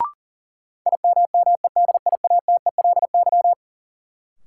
Morse code, heard as a keyed tone, about 40 wpm, 700 Hz. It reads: IMMEDIATELY
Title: Sentences from Top 1100 Words - New Word Drill Alt - 40wpm